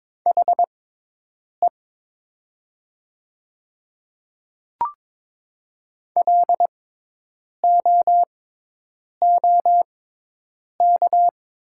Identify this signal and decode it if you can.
Morse code, heard as a keyed tone, about 22 wpm, spaced as for 8 wpm, 700 Hz. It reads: HE LOOK
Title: 6x Character Spacing - Top 100 Words - 22wpm